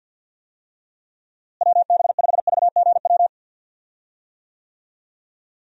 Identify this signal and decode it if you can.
Morse code, heard as a keyed tone, about 51 wpm, 700 Hz. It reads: WB5VGW
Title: USA Call Signs - 6 Characters - Repeated 3x - 50wpm